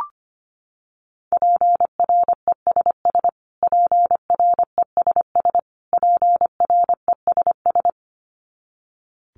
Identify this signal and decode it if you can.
Morse code, heard as a keyed tone, about 25 wpm, 700 Hz. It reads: PRESS PRESS PRESS
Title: Top 500 Words Repeated 25wpm